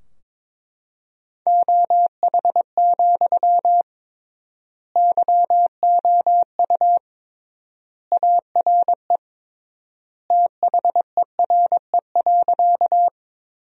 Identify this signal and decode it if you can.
Morse code, heard as a keyed tone, about 21 wpm, 700 Hz. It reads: OH, YOU ARE THERE.
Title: 3x Word-Spacing - Sentences from Top 500 Words - 22wpm